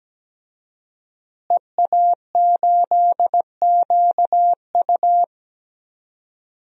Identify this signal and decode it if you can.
Morse code, heard as a keyed tone, about 17 wpm, 700 Hz. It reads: EA8QU
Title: DXCC-100 Call Signs - 5 Characters - Repeated 2x - 17wpm